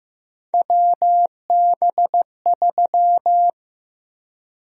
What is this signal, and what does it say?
Morse code, heard as a keyed tone, about 15 wpm, 700 Hz. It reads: WB3
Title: USA Call Sign Prefixes - Repeated 2x - 15wpm